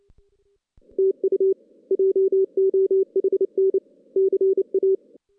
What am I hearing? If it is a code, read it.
Morse code, heard as a keyed tone, about 28 wpm, 380 Hz. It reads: TU JOHN CA
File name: 2016 JAN CW NAQP / 2016 Jan naqp k5ot - 0-6